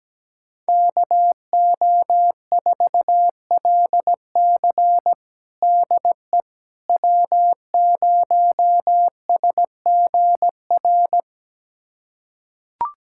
Morse code, heard as a keyed tone, about 17 wpm, 700 Hz. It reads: KO4LC DE W0SGR
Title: USA Callsign DE Callsign - 5 Characters - 17wpm